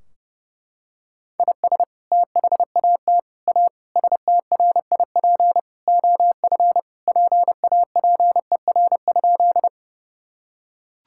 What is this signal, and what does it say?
Morse code, heard as a keyed tone, about 30 wpm, 700 Hz. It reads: IS THAT A STRIP OF PAPER?
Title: Sentences from Top 1600 Words 30wpm